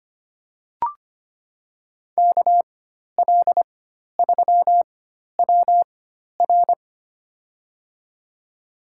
Morse code, heard as a keyed tone, about 25 wpm, spaced as for 12 wpm, 700 Hz. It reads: KL3WR
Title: USA Call Signs - 5 Characters - 4x Character Spacing - 25wpm